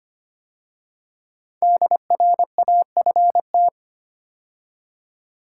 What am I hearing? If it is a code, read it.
Morse code, heard as a keyed tone, about 25 wpm, 700 Hz. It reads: DRAFT
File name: 5 Letter Words - First 3 Letters Given - 25wpm